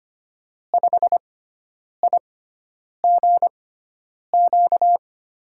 Morse code, heard as a keyed tone, about 25 wpm, spaced as for 9 wpm, 700 Hz. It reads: 5IGQ
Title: USA Call Signs - 5 Characters - 6x Character Spacing - 25wpm